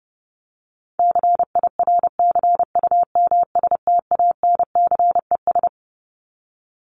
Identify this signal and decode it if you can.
Morse code, heard as a keyed tone, about 30 wpm, 700 Hz. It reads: CIRCUMSTANCES
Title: ES:S Suffix - Top 100 Words with Base Words 30wpm